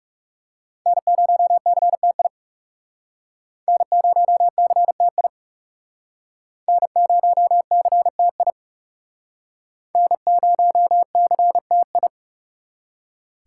Morse code, heard as a keyed tone, about 38 wpm, 700 Hz. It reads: N0CTI N0CTI N0CTI N0CTI